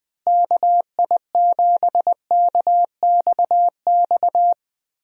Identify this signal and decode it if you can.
Morse code, heard as a keyed tone, about 20 wpm, 700 Hz. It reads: KI7KXX
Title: USA Call Signs - 6 Characters - Repeated 3x - 20wpm